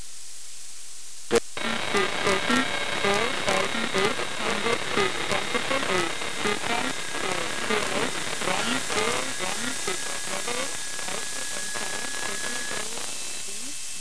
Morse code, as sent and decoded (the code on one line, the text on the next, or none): none